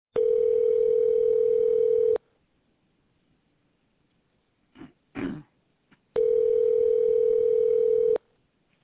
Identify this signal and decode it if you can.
Ringback tone (North America), 440 + 480 Hz.